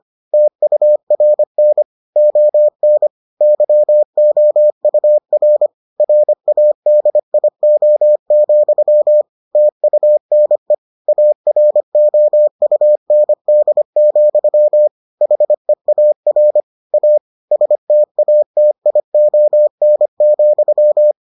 Morse code, heard as a keyed tone, about 25 wpm, 600 Hz. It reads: TURN ON YOUR RADIO, TUNE AROUND, HEAR A STATION,